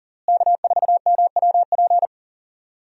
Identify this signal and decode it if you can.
Morse code, heard as a keyed tone, about 40 wpm, 700 Hz. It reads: K4MWP
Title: USA Call Signs - 5 Characters - Repeated 3x - 40wpm